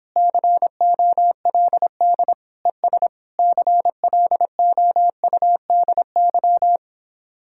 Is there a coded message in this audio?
Yes: Morse code from 0.2 to 6.8 s, heard as a keyed tone, about 29 wpm, 700 Hz. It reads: COLD ES CLOUDY